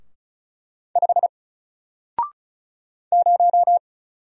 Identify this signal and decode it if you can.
Morse code, heard as a keyed tone, about 36 wpm, 700 Hz. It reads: 5 0
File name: Single-Letter-Number-35wpm